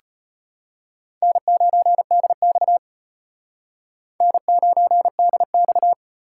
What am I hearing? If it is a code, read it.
Morse code, heard as a keyed tone, about 34 wpm, 700 Hz. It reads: N9DX N9DX